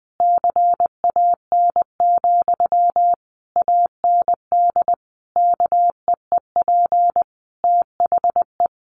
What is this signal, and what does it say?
Morse code, heard as a keyed tone, about 20 wpm, 700 Hz. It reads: CAN, AND KEEP THE
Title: Sentences from Top 1300 Words 20wpm